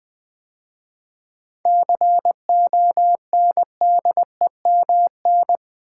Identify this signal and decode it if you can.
Morse code, heard as a keyed tone, about 20 wpm, 690 Hz. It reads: CONDEMN
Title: 7 Letter Words - First 3 Letters Given - 20wpm